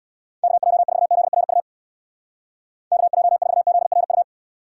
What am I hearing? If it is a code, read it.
Morse code, heard as a keyed tone, about 65 wpm, 700 Hz. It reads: KJ4ZRU KJ4ZRU